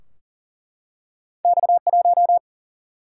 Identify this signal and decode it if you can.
Morse code, heard as a keyed tone, about 42 wpm, 700 Hz. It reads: X 1
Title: Single-Letter Single-Number 40wpm